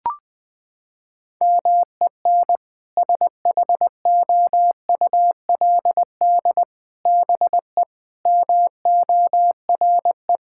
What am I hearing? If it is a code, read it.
Morse code, heard as a keyed tone, about 21 wpm, 700 Hz. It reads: MEN SHOULD BE MORE
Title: Sentences from Top 1600 Words 20wpm